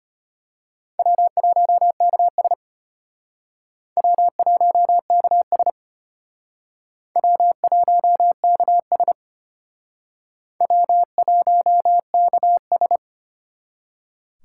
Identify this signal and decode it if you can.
Morse code, heard as a keyed tone, about 34 wpm, 700 Hz. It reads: W1KS W1KS W1KS W1KS